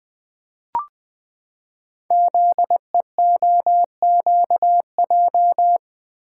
Morse code, heard as a keyed tone, about 20 wpm, 700 Hz. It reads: ZEOQJ
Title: Code Groups - 5 Letters - 20wpm